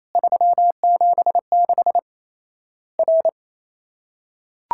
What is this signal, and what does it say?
Morse code, heard as a keyed tone, about 28 wpm, 700 Hz. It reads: 376 R
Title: Worked All Europe DX Contest - QTC Exchange - 28wpm